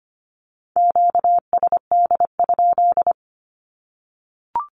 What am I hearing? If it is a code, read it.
Morse code, heard as a keyed tone, about 25 wpm, 700 Hz. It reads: QSD?
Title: Q-Codes Repeated 3x 25wpm